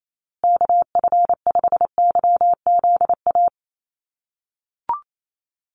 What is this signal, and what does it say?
Morse code, heard as a keyed tone, about 28 wpm, 700 Hz. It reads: KF5YZA
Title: USA Call Signs Repeated 3x 28wpm